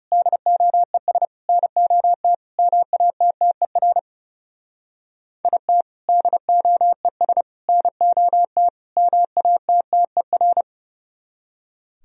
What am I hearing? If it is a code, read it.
Morse code, heard as a keyed tone, about 37 wpm, 700 Hz. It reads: DOES NOT MATTER IT DOES NOT MATTER
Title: Speed-Racing - Sets of 4 Words 30wpm